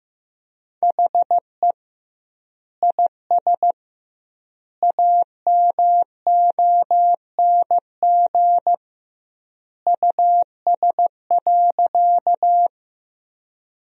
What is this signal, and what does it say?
Morse code, heard as a keyed tone, about 15 wpm, 700 Hz. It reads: HE IS AMONG US.
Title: 2x Word-Spacing - Sentences from Top 500 Words - 15wpm